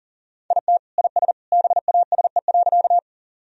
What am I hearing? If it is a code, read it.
Morse code, heard as a keyed tone, about 40 wpm, 700 Hz. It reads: IT IS BASE.
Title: Sentences from Top 1000 Words - Review Part 1 - 40wpm